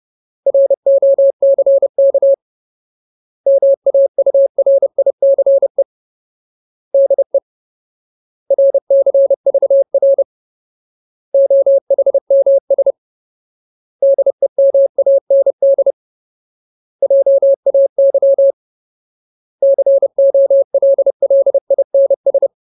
Morse code, heard as a keyed tone, about 30 wpm, 550 Hz. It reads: ROCK MAURICE DE RCVR OHMS DEMAND JAY COLLINS
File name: Ultimate Word List / 30WPM Ultimate